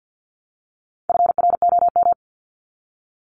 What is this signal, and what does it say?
Morse code, heard as a keyed tone, about 50 wpm, 700 Hz. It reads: FROM
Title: Top 500 Words 50wpm